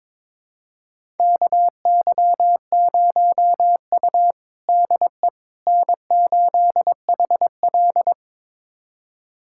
Morse code, heard as a keyed tone, about 22 wpm, 700 Hz. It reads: KY0U DE N8HL
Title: USA Callsign DE Callsign - 4 Characters - 22wpm